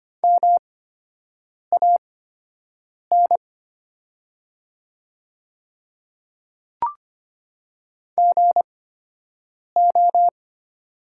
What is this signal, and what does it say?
Morse code, heard as a keyed tone, about 25 wpm, spaced as for 7 wpm, 700 Hz. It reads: MAN GO
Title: 8x Character Spacing - Top 200 Words - 25wpm